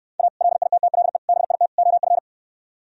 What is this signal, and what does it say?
Morse code, heard as a keyed tone, about 67 wpm, 700 Hz. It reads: A LITTLE BIT OF